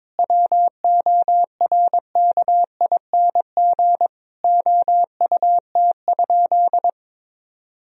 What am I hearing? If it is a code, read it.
Morse code, heard as a keyed tone, about 22 wpm, 700 Hz. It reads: WORKING OUT?
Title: Sentences from Top 1000 Words - Review Part 4 - 22wpm